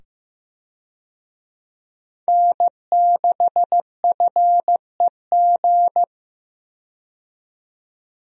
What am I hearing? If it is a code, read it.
Morse code, heard as a keyed tone, about 15 wpm, 700 Hz. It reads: N6FEG
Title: USA Call Signs - 5 Characters - 15wpm